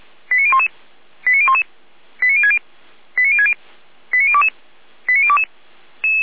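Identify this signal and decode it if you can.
Five-tone selective call (ZVEI), heard as quick runs of five tones, tones of about 70 ms. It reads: 89011 89011 89060 89060 89022 89022
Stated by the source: ZVEI five-tone call 89001 received on SDR, logged by fire brigade Feuerwehr Orlamünde.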